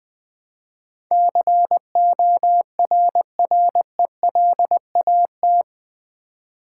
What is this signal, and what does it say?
Morse code, heard as a keyed tone, about 20 wpm, 700 Hz. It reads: CORRELAT
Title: Build A Word - 4 Letters At A Time - 11-Letter Words - 20wpm